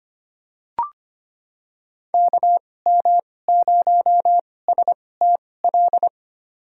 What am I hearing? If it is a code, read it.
Morse code, heard as a keyed tone, about 25 wpm, spaced as for 18 wpm, 700 Hz. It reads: KM0STL
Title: USA Call Signs - 6 Characters - 2x Character Spacing - 25wpm